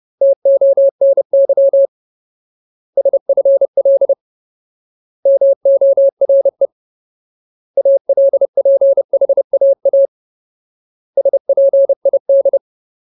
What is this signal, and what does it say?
Morse code, heard as a keyed tone, about 30 wpm, 550 Hz. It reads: TONY SFL MORE ALPHAA SPID
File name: Ultimate Word List / 30WPM Ultimate